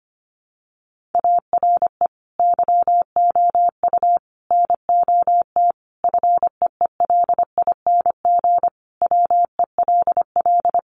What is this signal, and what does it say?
Morse code, heard as a keyed tone, about 25 wpm, 700 Hz. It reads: ARE YOU NOT FEELING WELL
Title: Sentences from Top 1200 Words 25wpm